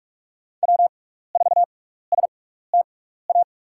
Morse code, heard as a keyed tone, about 43 wpm, spaced as for 15 wpm, 700 Hz. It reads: W4STA
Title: USA Call Signs - 5 Characters - 6x Character Spacing - 45wpm